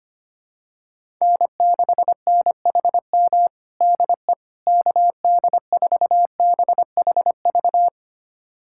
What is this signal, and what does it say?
Morse code, heard as a keyed tone, about 25 wpm, 700 Hz. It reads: N6NHM DE KD4BHV
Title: USA Callsign DE Callsign 25wpm